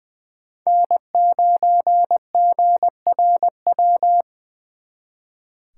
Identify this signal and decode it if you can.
Morse code, heard as a keyed tone, about 25 wpm, 700 Hz. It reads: N9GRW
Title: Speed Racing - USA 1x3 Call Signs 20wpm